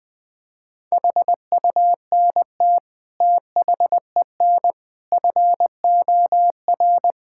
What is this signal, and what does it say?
Morse code, heard as a keyed tone, about 20 wpm, 700 Hz. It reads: HUNT THEN FOR
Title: Sentences from Top 1600 Words 20wpm